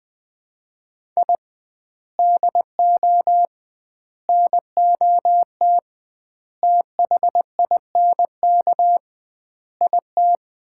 Morse code, heard as a keyed tone, about 21 wpm, 700 Hz. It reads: I DO NOT THINK IT
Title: 2x Word-Spacing - Sets of 5 Words - 20wpm